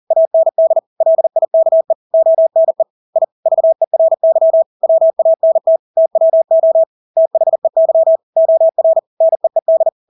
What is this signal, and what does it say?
Morse code, heard as a keyed tone, about 40 wpm, 650 Hz. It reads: AND LIKE ONE I VERY WANT TWO THEY OR NEED
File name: Engelske ord 40-40 wpm